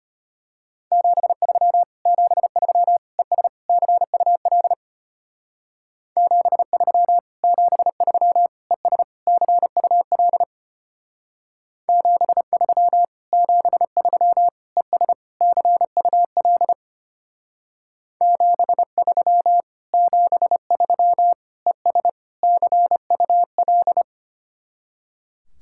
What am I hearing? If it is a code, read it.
Morse code, heard as a keyed tone, about 33 wpm, 700 Hz. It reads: 73 73 ES CUL 73 73 ES CUL 73 73 ES CUL 73 73 ES CUL